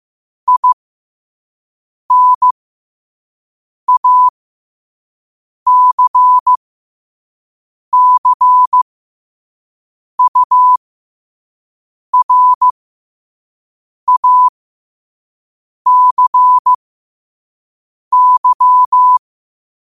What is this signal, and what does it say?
Morse code, heard as a keyed tone, about 15 wpm, spaced as for 5 wpm, 1000 Hz. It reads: INACCURACY
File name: qotdfiles / 5wpm